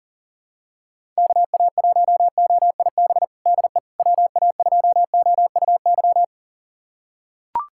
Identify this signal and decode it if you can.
Morse code, heard as a keyed tone, about 37 wpm, 700 Hz. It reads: KA1OIB DE WA2OUY
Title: USA Callsign DE Callsign 40wpm